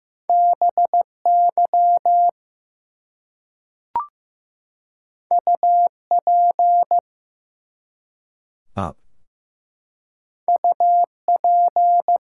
Morse code, heard as a keyed tone, about 15 wpm, 700 Hz. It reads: BY UP UP
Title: Two Letter Words - 15wpm